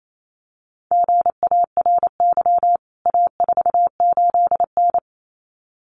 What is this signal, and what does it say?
Morse code, heard as a keyed tone, about 28 wpm, 700 Hz. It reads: GARY A48N
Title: CWT Exchange 28wpm